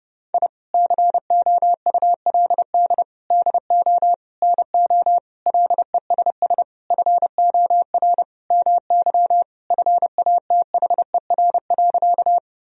Morse code, heard as a keyed tone, about 30 wpm, 700 Hz. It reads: I COULD DO NO LESS FOR MY FATHER.
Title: Sentences from Top 1000 Words - Review Part 2 - 30wpm